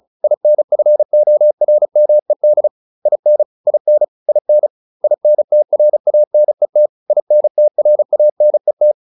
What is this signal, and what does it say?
Morse code, heard as a keyed tone, about 34 wpm, 600 Hz. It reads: INFORMED IN IN IN INTRANET INTRANET